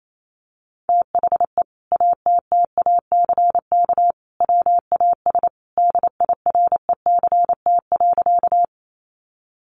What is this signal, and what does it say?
Morse code, heard as a keyed tone, about 28 wpm, 700 Hz. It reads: THE ATTACK WAS DIRECT.
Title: Sentences from Top 1500 Words 28wpm